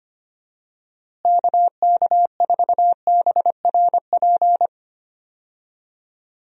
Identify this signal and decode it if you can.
Morse code, heard as a keyed tone, about 25 wpm, 700 Hz. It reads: KK4BRP
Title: USA Call Signs - 6 Characters - Repeated 2x - 25wpm